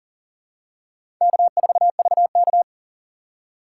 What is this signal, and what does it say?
Morse code, heard as a keyed tone, about 40 wpm, 700 Hz. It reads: K4VK
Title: USA Call Signs - 4 Characters - 3x Character Spacing - 40wpm